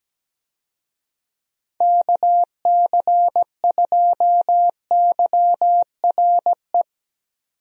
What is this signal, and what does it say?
Morse code, heard as a keyed tone, about 17 wpm, 700 Hz. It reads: KC2YRE
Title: USA Call Signs - 6 Characters - 17wpm